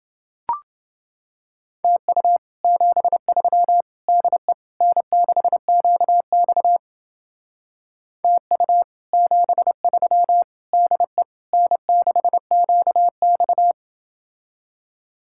Morse code, heard as a keyed tone, about 28 wpm, 700 Hz. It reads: TU 73 DE N6QX TU 73 DE N6QX